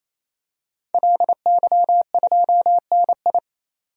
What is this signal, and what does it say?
Morse code, heard as a keyed tone, about 28 wpm, 700 Hz. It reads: LY2NI
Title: DXCC-100 Call Signs - 5 Characters - 28wpm